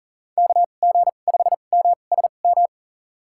Morse code, heard as a keyed tone, about 40 wpm, spaced as for 29 wpm, 700 Hz. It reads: KG5MSM
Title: USA Call Signs - 2x Character Spacing - 40wpm